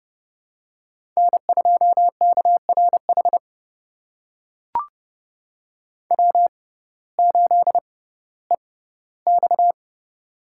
Morse code, heard as a keyed tone, about 32 wpm, spaced as for 17 wpm, 700 Hz. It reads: N2KRH W8EX
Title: USA Call Signs - 6x Character Spacing - 30wpm